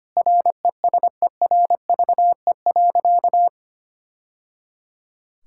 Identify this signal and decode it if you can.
Morse code, heard as a keyed tone, about 25 wpm, 700 Hz. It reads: RESERVE.
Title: Sentences from Top 1000 Words - Review Part 6 - 25wpm